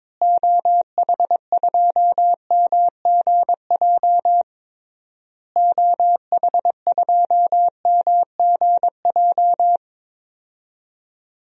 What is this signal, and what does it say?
Morse code, heard as a keyed tone, about 21 wpm, 700 Hz. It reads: OH2MGJ OH2MGJ
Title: DXCC-100 Call Signs - 6 Characters - Repeated 3x - 22wpm